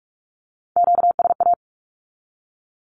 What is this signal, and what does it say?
Morse code, heard as a keyed tone, about 46 wpm, 700 Hz. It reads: QSA
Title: Q-Codes with Spoken Meaning - 45wpm